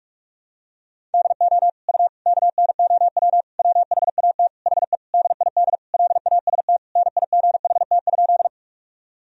Morse code, heard as a keyed tone, about 46 wpm, 700 Hz. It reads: DO U KNOW WHAT HE DID LAST NIGHT?
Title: Sentences from Top 300 Words and Abbreviations 45wpm